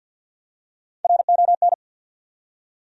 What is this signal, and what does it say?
Morse code, heard as a keyed tone, about 51 wpm, 700 Hz. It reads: RON
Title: USA Common Names - Max 4 Letters - 50wpm